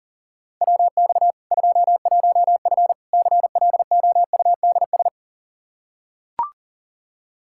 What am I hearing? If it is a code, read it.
Morse code, heard as a keyed tone, about 37 wpm, 700 Hz. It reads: WX 21F CLOUDS
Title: K1USN SST QSO Practice 40wpm